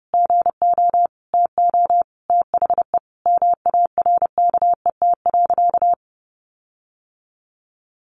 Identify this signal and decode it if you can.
Morse code, heard as a keyed tone, about 30 wpm, 700 Hz. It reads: GO TO THE MARKET.
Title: Sentences from Top 1400 Words 30wpm